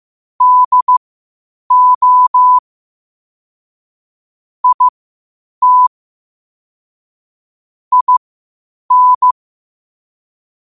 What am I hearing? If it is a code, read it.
Morse code, heard as a keyed tone, about 15 wpm, spaced as for 8 wpm, 1000 Hz. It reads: DO IT IN
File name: qotdfiles / 8wpm